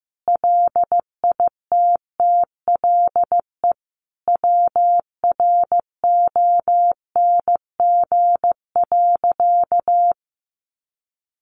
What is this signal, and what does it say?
Morse code, heard as a keyed tone, about 15 wpm, 700 Hz. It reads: LITTLE WRONG.
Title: Sentences from Top 900 Words 15wpm